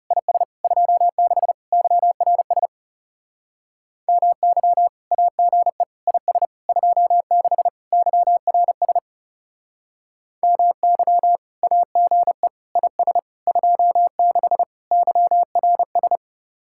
Morse code, heard as a keyed tone, about 37 wpm, 700 Hz. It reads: IS 26 YRS MY AGE IS 26 YRS MY AGE IS 26 YRS
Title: Speed-Racing Common QSO Phrases - 30wpm